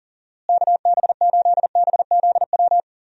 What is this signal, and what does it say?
Morse code, heard as a keyed tone, about 40 wpm, 700 Hz. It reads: KB8BZW
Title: USA Call Signs - 6 Characters - Repeated 3x - 40wpm